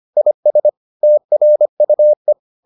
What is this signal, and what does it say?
Morse code, heard as a keyed tone, about 25 wpm, 600 Hz. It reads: IS TRUE